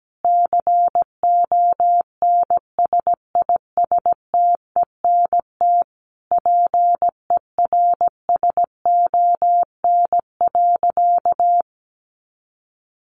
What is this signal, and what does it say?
Morse code, heard as a keyed tone, about 17 wpm, 700 Hz. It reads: CONSISTENT PERSON.
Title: Sentences from Top 1500 Words 17wpm